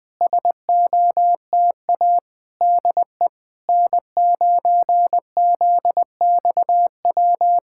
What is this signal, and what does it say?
Morse code, heard as a keyed tone, about 21 wpm, 690 Hz. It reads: SOTA DE N9ZXW
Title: SOTA QSO Practice 20wpm